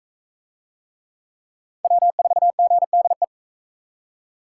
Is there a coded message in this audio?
Yes: Morse code from 1.8 to 3.2 s, heard as a keyed tone, about 37 wpm, 700 Hz. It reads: W4GDE